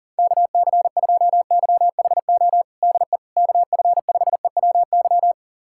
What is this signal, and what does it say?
Morse code, heard as a keyed tone, about 37 wpm, 700 Hz. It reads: KC2YHO DE KF5EWY
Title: USA Callsign DE Callsign - 6 Characters - 40wpm